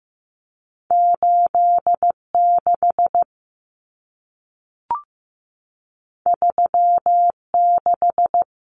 Morse code, heard as a keyed tone, about 15 wpm, 700 Hz. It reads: 86 36
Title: Common Cut Numbers - 2-Digit 15wpm